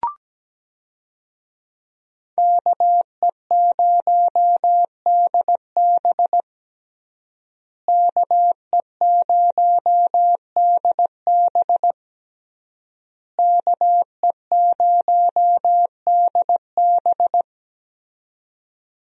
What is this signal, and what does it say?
Morse code, heard as a keyed tone, about 17 wpm, 700 Hz. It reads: KE0DB KE0DB KE0DB
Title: USA Call Signs - 5 Characters - Repeated 3x - 17wpm